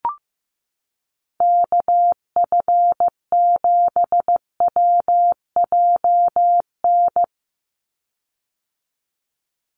Morse code, heard as a keyed tone, about 15 wpm, 700 Hz. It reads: KF7WJN
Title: USA Call Signs 15wpm